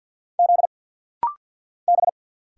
Morse code, heard as a keyed tone, about 52 wpm, 700 Hz. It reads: Z B
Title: Single Letters - Rapid-Fire 50wpm